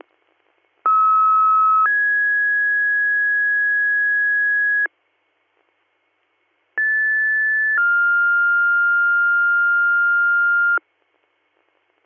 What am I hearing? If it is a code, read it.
Two-tone page, heard as a one-second tone, then a three-second tone, about 1290 then 1650 Hz, about 1750 then 1400 Hz. It is heard from 0.9 to 10.8 s.